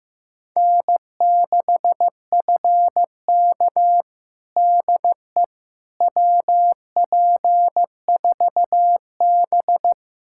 Morse code, heard as a keyed tone, about 16 wpm, 700 Hz. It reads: N6FK DE WP4B
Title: USA Callsign DE Callsign - 4 Characters - 15wpm